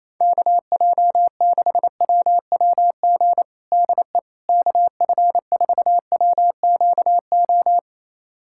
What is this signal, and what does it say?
Morse code, heard as a keyed tone, about 28 wpm, 700 Hz. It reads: KJ6WWG DE KF4WQO
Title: USA Callsign DE Callsign - 6 Characters - 28wpm